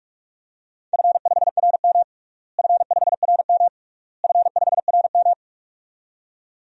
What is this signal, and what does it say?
Morse code, heard as a keyed tone, about 41 wpm, 700 Hz. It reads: F5RM F5RM F5RM
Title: DXCC-100 Call Signs - 4 Characters - Repeated 3x - 45wpm